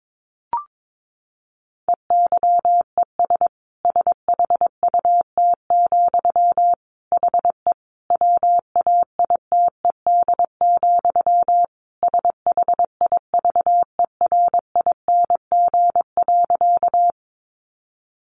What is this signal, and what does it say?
Morse code, heard as a keyed tone, about 22 wpm, 700 Hz. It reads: EYES SHUT, HE WAITED, SHIVERING.
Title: A Sound Of Thunder 22wpm